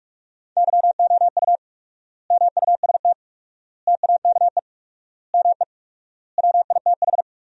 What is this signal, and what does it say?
Morse code, heard as a keyed tone, about 45 wpm, 700 Hz. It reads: YOU MUST TAKE ME WITH
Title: 4x Word-Spacing - Sentences from Top 200 Words - 45wpm